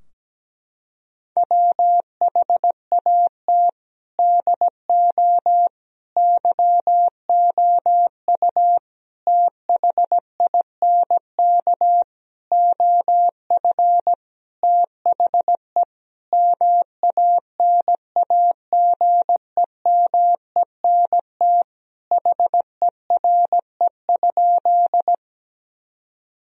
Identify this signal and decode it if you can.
Morse code, heard as a keyed tone, about 17 wpm, 700 Hz. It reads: WHAT DO YOU THINK OF THE MANAGEMENT HERE?
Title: Sentences from Top 1000 Words - Review Part 4 - 17wpm